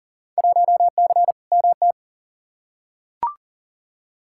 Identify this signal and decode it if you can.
Morse code, heard as a keyed tone, about 40 wpm, 700 Hz. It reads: 1C MT
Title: Field Day Exchanges Encore - 40wpm